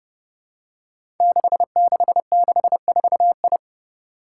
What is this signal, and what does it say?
Morse code, heard as a keyed tone, about 28 wpm, 700 Hz. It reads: 6664I
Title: Five Challenging Letter-Number Combinations 30wpm